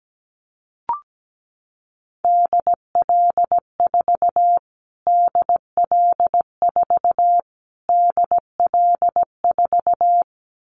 Morse code, heard as a keyed tone, about 17 wpm, 700 Hz. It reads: DL4 DL4 DL4
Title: Callsign Prefixes for Top 25 DXCC - Repeated 3x - 17wpm